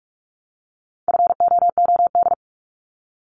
Morse code, heard as a keyed tone, about 46 wpm, 700 Hz. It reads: FOOD
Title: Multi-letter Phonemes 45wpm